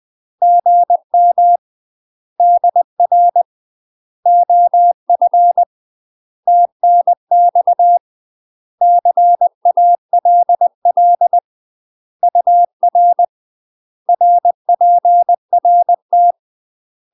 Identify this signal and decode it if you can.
Morse code, heard as a keyed tone, about 20 wpm, 700 Hz. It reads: GM DR OF TNX CALL UR RPRT
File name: UK MASTER QSOs TRAINING / 20WPM / UK MASTER QSOs 0002